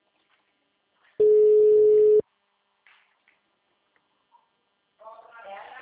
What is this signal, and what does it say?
Ringback tone (Europe, 425 Hz), with speech.